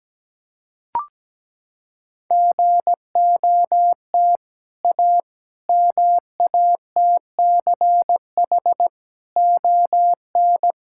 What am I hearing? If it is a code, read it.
Morse code, heard as a keyed tone, about 17 wpm, 700 Hz. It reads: GOT A MATCH ON
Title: Sentences from Top 1600 Words 17wpm